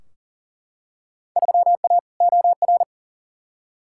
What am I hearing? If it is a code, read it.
Morse code, heard as a keyed tone, about 40 wpm, 700 Hz. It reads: 3A OR
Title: Field Day Exchanges Encore - 40wpm